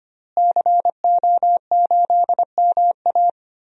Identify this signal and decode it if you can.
Morse code, heard as a keyed tone, about 25 wpm, 700 Hz. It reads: CO8MA
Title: DXCC-100 Call Signs - 5 Characters - Repeated 2x - 25wpm